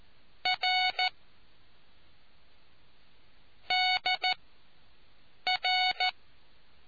Morse code, heard as a keyed tone, about 14 wpm, spaced as for 6 wpm, 740 Hz. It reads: R DR